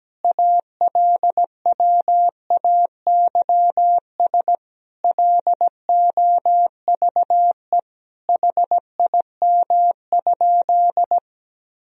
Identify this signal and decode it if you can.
Morse code, heard as a keyed tone, about 17 wpm, 700 Hz. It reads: ALWAYS LOVE HIM?
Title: Sentences from Top 400 Words and Abbreviations 17wpm